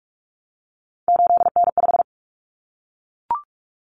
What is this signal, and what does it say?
Morse code, heard as a keyed tone, about 44 wpm, 700 Hz. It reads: 8N5
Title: Common Cut Numbers - 3-Digit 45wpm